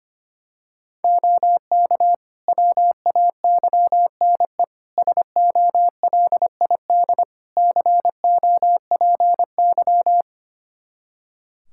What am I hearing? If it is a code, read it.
Morse code, heard as a keyed tone, about 25 wpm, 700 Hz. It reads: OK WAYNE SOLID COPY
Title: Practice QSO - Exchange 3 - More Info - 25wpm